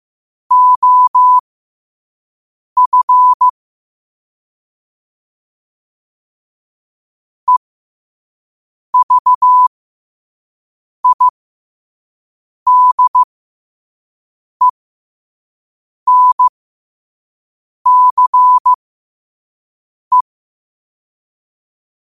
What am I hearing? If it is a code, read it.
Morse code, heard as a keyed tone, about 15 wpm, spaced as for 5 wpm, 1000 Hz. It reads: OF EVIDENCE
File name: qotdfiles / 5wpm